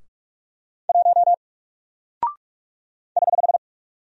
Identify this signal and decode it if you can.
Morse code, heard as a keyed tone, about 45 wpm, 700 Hz. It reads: 1 <HH>
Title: Single Character (Letter:Number:Punctuation:Prosign) Alt 45wpm